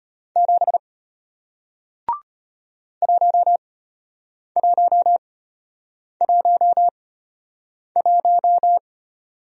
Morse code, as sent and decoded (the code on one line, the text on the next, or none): --... / .---- .---- .---- .----
7 1111